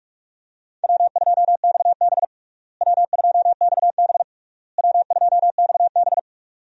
Morse code, heard as a keyed tone, about 46 wpm, 700 Hz. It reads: W2XB W2XB W2XB
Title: USA Call Signs - 4 Characters - Repeated 3x - 45wpm